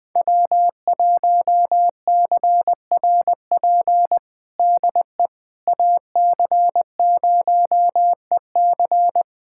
Morse code, heard as a keyed tone, about 20 wpm, 700 Hz. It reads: W1CRP DE AC0EC
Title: USA Callsign DE Callsign - 5 Characters - 20wpm